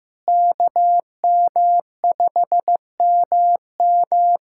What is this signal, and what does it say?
Morse code, heard as a keyed tone, about 15 wpm, 700 Hz. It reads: KM5MM
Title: USA Callsign DE Callsign 15wpm